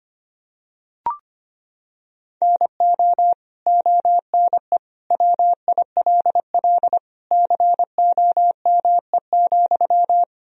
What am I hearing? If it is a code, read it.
Morse code, heard as a keyed tone, about 25 wpm, 700 Hz. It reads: NO ONE WILL COME,
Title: Sentences from Top 100 Words and Abbreviations 25wpm